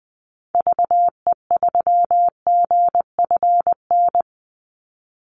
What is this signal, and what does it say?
Morse code, heard as a keyed tone, about 20 wpm, 700 Hz. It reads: VE3GFN
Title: Call Signs Repeated 2x - 20wpm